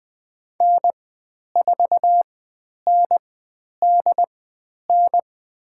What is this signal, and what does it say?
Morse code, heard as a keyed tone, about 20 wpm, spaced as for 10 wpm, 700 Hz. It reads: N4NDN
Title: 095 Lesson 16g - Letter c - Call Signs 20wpm